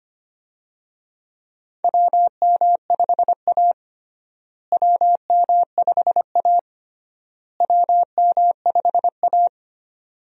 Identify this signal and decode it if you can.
Morse code, heard as a keyed tone, about 25 wpm, 700 Hz. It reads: WM5A WM5A WM5A
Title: USA Call Signs - 4 Characters - Repeated 3x - 25wpm